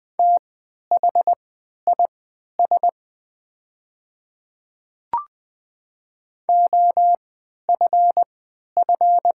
Morse code, heard as a keyed tone, about 20 wpm, spaced as for 11 wpm, 700 Hz. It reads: THIS OFF
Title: 3x Character Spacing - Top 200 Words - 20wpm